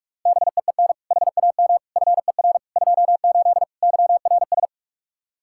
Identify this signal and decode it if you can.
Morse code, heard as a keyed tone, about 45 wpm, 700 Hz. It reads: BEEN HAM FER 28 YRS
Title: Practice QSO - Exchange 3 - More Info - 45wpm